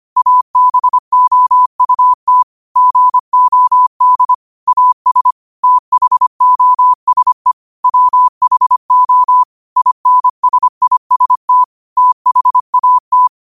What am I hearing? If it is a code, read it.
Morse code, heard as a keyed tone, about 25 wpm, 1000 Hz. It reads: ABOUT GOD AS THOSE WHO INSIST THAT
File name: qotdfiles / 25wpm